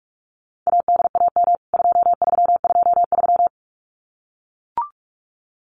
Morse code, heard as a keyed tone, about 44 wpm, 700 Hz. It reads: ADAM 2323
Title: CWT Exchange 45wpm